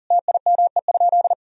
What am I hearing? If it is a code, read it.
Morse code, heard as a keyed tone, about 40 wpm, 700 Hz. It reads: TIME?
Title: Sentences from Top 1000 Words - Review Part 3 - 40wpm